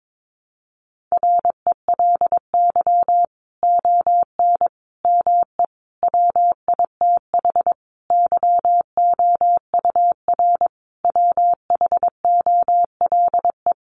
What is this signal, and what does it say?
Morse code, heard as a keyed tone, about 22 wpm, 700 Hz. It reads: RELY ON ME WITH YOUR WHOLE
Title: Sentences from Top 1200 Words 22wpm